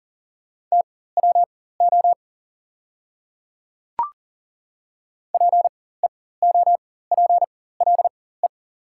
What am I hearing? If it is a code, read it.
Morse code, heard as a keyed tone, about 39 wpm, spaced as for 19 wpm, 690 Hz. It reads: TWO PEOPLE
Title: 4x Character Spacing - Top 100 Words - 40wpm